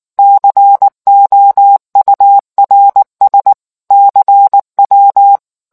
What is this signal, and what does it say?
Morse code, heard as a keyed tone, about 19 wpm, 800 Hz. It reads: COURS CW